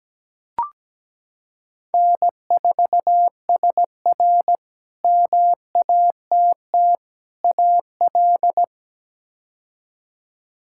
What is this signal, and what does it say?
Morse code, heard as a keyed tone, about 17 wpm, 700 Hz. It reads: N4SR MATT AL
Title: K1USN SST QSO Practice - 17wpm